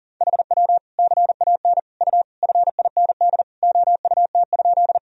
Morse code, heard as a keyed tone, about 40 wpm, 700 Hz. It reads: HW CAN U FIND OUT?
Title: Sentences from Top 200 Words and Abbreviations 40wpm